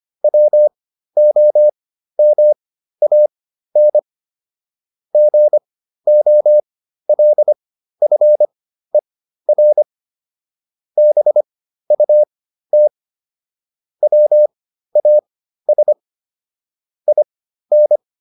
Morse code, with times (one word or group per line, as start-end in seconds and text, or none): 0.2-4.0 WOMAN
5.1-9.8 GOLFER
11.0-12.9 BUT
14.0-15.9 WAS
17.1-18.0 IN